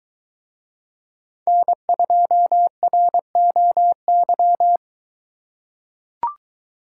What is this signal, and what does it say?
Morse code, heard as a keyed tone, about 23 wpm, 700 Hz. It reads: N2ROY